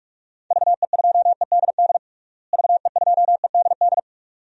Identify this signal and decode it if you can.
Morse code, heard as a keyed tone, about 46 wpm, 700 Hz. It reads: VE2EDD VE2EDD
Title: DXCC-100 Call Signs - 6 Characters - Repeated 2x - 45wpm